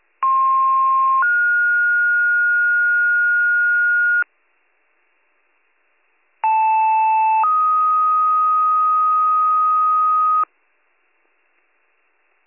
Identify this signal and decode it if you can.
Two-tone page, heard as a one-second tone, then a three-second tone, about 1060 then 1430 Hz, about 880 then 1250 Hz. It is heard from 0.2 to 10.4 s.